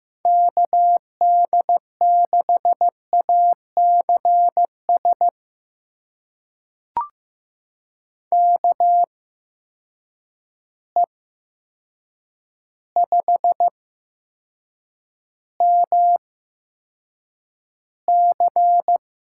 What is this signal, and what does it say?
Morse code, heard as a keyed tone, about 15 wpm, spaced as for 7 wpm, 700 Hz. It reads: KD6ACS KE5MC
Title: USA Call Signs - 6 Characters - 8x Character Spacing - 15wpm